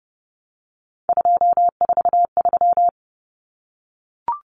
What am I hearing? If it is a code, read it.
Morse code, heard as a keyed tone, about 30 wpm, 700 Hz. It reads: 243
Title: Common Cut Numbers - 3-Digit 30wpm